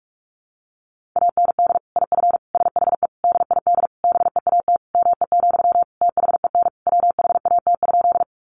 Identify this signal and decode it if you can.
Morse code, heard as a keyed tone, about 44 wpm, 700 Hz. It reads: AND IF SHE DID BEAT ME, THEN WHAT?
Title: Sentences from Top 1300 Words 45wpm